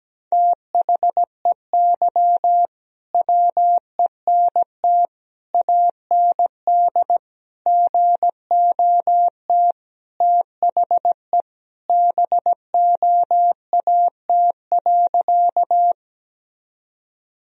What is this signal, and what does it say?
Morse code, heard as a keyed tone, about 17 wpm, 700 Hz. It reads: THEY WENT AND GOT THE BOAT.
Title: Sentences from Top 1000 Words - Review Part 3 - 17wpm